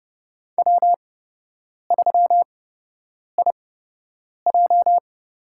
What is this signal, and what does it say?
Morse code, heard as a keyed tone, about 31 wpm, spaced as for 8 wpm, 700 Hz. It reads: W3IJ
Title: USA Call Signs - 8x Character Spacing - 30wpm